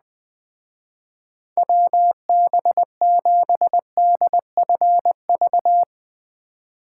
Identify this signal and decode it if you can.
Morse code, heard as a keyed tone, about 20 wpm, 700 Hz. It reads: WB7DFV
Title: Speed Racing - USA Call Signs 17wpm